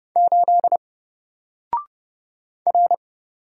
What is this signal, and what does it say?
Morse code, heard as a keyed tone, about 30 wpm, 700 Hz. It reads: 8 R
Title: Single Character (Letter:Number:Punctuation:Prosign) Alt 30wpm